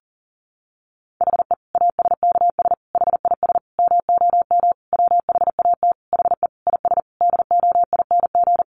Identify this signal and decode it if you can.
Morse code, heard as a keyed tone, about 40 wpm, 700 Hz. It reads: HE ASKS HIS MOM WHAT HE IS DOING
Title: Easy Phrases 40wpm